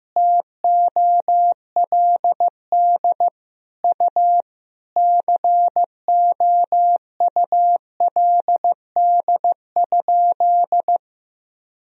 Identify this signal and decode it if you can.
Morse code, heard as a keyed tone, about 15 wpm, 700 Hz. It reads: TOLD U COULD?
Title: Sentences from Top 400 Words and Abbreviations 15wpm